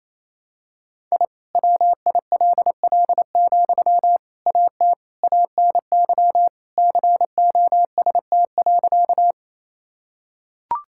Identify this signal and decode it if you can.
Morse code, heard as a keyed tone, about 27 wpm, 700 Hz. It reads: I WILL, AT ANY COST.
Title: Sentences from Top 1000 Words - Review Part 5 - 28wpm